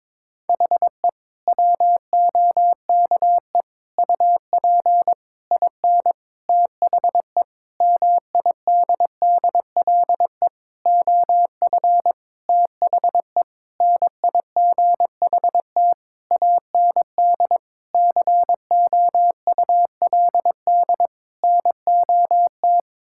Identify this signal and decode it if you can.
Morse code, heard as a keyed tone, about 22 wpm, 700 Hz. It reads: HE WOKE UP IN THE MIDDLE OF THE NIGHT AND COULD NOT
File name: Sentences with 7-Word Phrases - 22wpm